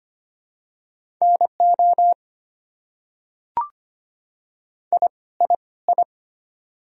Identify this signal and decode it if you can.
Morse code, heard as a keyed tone, about 25 wpm, 700 Hz. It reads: NO I I I
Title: Top 100 Words Repeated 25wpm